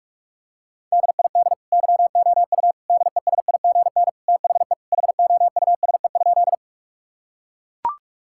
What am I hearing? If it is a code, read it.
Morse code, heard as a keyed tone, about 41 wpm, 700 Hz. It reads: DID YOU DESIGN THE HOUSE?
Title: Sentences from Top 1000 Words - Review Part 5 - 45wpm